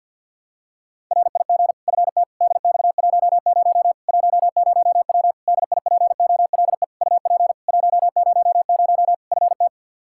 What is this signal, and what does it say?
Morse code, heard as a keyed tone, about 48 wpm, 700 Hz. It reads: RIG FT DX10 10W DIPOLE UP 100 FT
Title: Practice QSO - Exchange 2 - Info - 50wpm